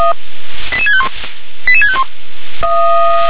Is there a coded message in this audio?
Yes: five-tone selective call (ZVEI) from 0.7 to 2.0 s, heard as quick runs of five tones, tones of about 70 ms. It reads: 88651 88651